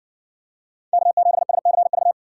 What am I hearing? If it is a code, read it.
Morse code, heard as a keyed tone, about 57 wpm, 700 Hz. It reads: K7SCV